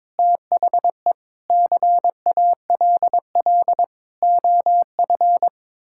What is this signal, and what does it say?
Morse code, heard as a keyed tone, about 22 wpm, 700 Hz. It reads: THE CALL OF
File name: practice / 22wpm / Sets of 8 Words - 22wpm